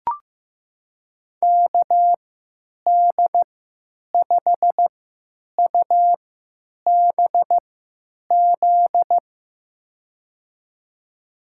Morse code, heard as a keyed tone, about 15 wpm, spaced as for 9 wpm, 700 Hz. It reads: KD5UBZ
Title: USA Call Signs - 6 Characters - 3x Character Spacing - 15wpm